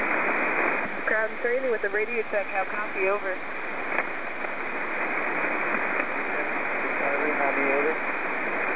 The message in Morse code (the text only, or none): none